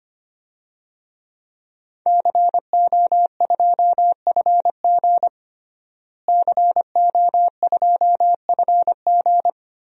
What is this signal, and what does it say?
Morse code, heard as a keyed tone, about 25 wpm, 700 Hz. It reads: CO2FG CO2FG
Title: DXCC-100 Call Signs - 5 Characters - Repeated 2x - 25wpm